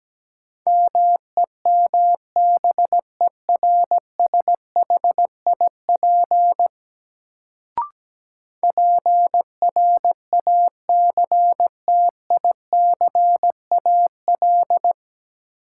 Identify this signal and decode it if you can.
Morse code, heard as a keyed tone, about 17 wpm, 700 Hz. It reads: MEMBERSHIP PRACTICAL
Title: Sentences from Top 1400 Words - New Word Drill - 17wpm